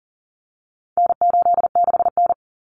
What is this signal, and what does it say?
Morse code, heard as a keyed tone, about 40 wpm, 700 Hz. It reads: N86N
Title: Common Cut Numbers - 4-Digit 40wpm